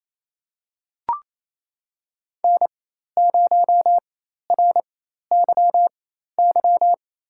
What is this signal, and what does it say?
Morse code, heard as a keyed tone, about 28 wpm, spaced as for 13 wpm, 700 Hz. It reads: N0RYY
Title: USA Call Signs - 4x Character Spacing - 28wpm